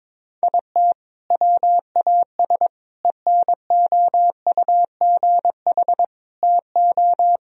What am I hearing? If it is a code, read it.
Morse code, heard as a keyed tone, about 22 wpm, 700 Hz. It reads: IT WAS ENOUGH TO
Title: Sentences from Top 1000 Words - Review Part 4 - 22wpm